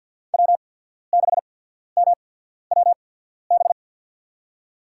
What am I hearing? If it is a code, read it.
Morse code, heard as a keyed tone, about 54 wpm, spaced as for 14 wpm, 700 Hz. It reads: W6MWB
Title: USA Call Signs - 5 Characters - 8x Character Spacing - 50wpm